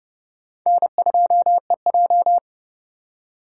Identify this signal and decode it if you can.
Morse code, heard as a keyed tone, about 29 wpm, 700 Hz. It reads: N2EJ